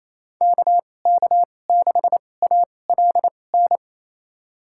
Morse code, heard as a keyed tone, about 28 wpm, spaced as for 20 wpm, 700 Hz. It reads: KK6ALN
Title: USA Call Signs - 6 Characters - 2x Character Spacing - 28wpm